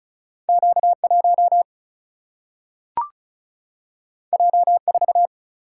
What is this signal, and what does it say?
Morse code, heard as a keyed tone, about 36 wpm, 700 Hz. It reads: Q 1 J 4
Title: Single-Letter Single-Number 35wpm